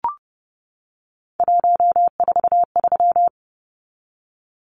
Morse code, heard as a keyed tone, about 30 wpm, 700 Hz. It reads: 143
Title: Numbers - 3-Digit 30wpm